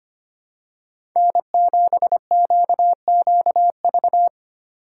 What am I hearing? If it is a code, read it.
Morse code, heard as a keyed tone, about 25 wpm, 700 Hz. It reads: N7QQV
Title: USA Call Signs - 5 Characters - 2x Character Spacing - 25wpm